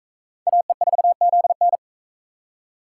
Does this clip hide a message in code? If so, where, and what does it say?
Yes: Morse code from 0.5 to 1.8 s, heard as a keyed tone, about 43 wpm, 700 Hz. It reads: AE4ZN